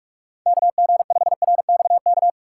Morse code, heard as a keyed tone, about 46 wpm, 700 Hz. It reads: KG5RXK
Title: USA Call Signs - 6 Characters - 2x Character Spacing - 45wpm